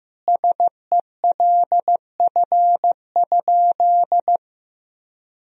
Morse code, heard as a keyed tone, about 15 wpm, 700 Hz. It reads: SELF?
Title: Sentences from Top 1000 Words - Review Part 1 - 15wpm